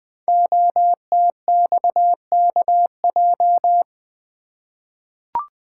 Morse code, heard as a keyed tone, about 20 wpm, 700 Hz. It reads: OTXKJ
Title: Code Groups - 5 Letter:Numbers - 20wpm